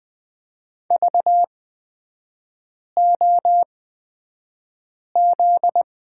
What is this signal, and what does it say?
Morse code, heard as a keyed tone, about 20 wpm, spaced as for 5 wpm, 700 Hz. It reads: VOZ
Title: 255 Lesson 41e - Period - Three Characters 20wpm